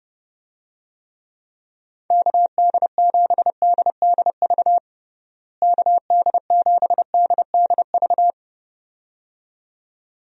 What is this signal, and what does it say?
Morse code, heard as a keyed tone, about 32 wpm, 700 Hz. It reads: KD7DDV KD7DDV
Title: USA Call Signs - 6 Characters - Repeated 2x - 30wpm